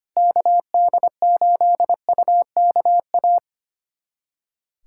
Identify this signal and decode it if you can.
Morse code, heard as a keyed tone, about 25 wpm, 700 Hz. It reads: KD8UKA
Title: USA Callsign DE Callsign - 6 Characters - 25wpm